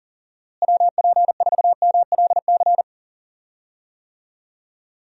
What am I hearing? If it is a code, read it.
Morse code, heard as a keyed tone, about 40 wpm, 700 Hz. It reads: WP4MLC
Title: USA Call Signs - 6 Characters - Repeated 2x - 40wpm